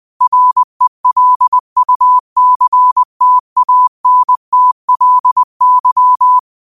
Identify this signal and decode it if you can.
Morse code, heard as a keyed tone, about 20 wpm, 1000 Hz. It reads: RELUCTANTLY